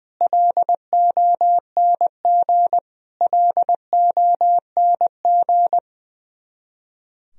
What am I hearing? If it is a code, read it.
Morse code, heard as a keyed tone, about 20 wpm, 700 Hz. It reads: LONG LONG
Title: Top 100 Words Repeated 20wpm